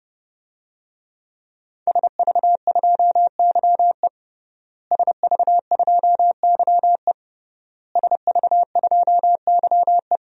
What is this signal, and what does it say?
Morse code, heard as a keyed tone, about 30 wpm, 700 Hz. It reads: SV2YE SV2YE SV2YE
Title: DXCC-100 Call Signs - 5 Characters - Repeated 3x - 30wpm